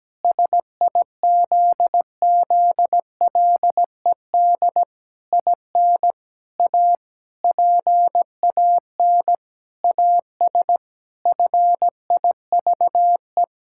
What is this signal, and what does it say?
Morse code, heard as a keyed tone, about 17 wpm, 700 Hz. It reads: SIZZLED IN A PAN AS FIVE
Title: Pangram Challenge v2 - 17wpm